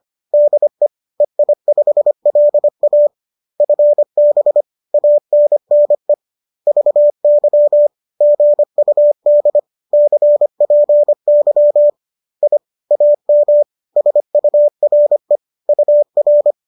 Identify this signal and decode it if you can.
Morse code, heard as a keyed tone, about 25 wpm, 590 Hz. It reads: DE EI5LA FB ANNE VY GUD CPY I AM SURE UR